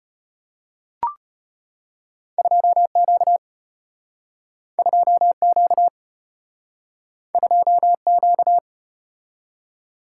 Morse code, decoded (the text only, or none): 2Q 2Q 2Q